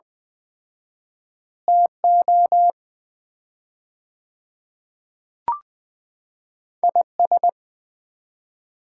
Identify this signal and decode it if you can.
Morse code, heard as a keyed tone, about 21 wpm, 700 Hz. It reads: TO IS
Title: 4x Word-Spacing - Sets of 2 Words - 20wpm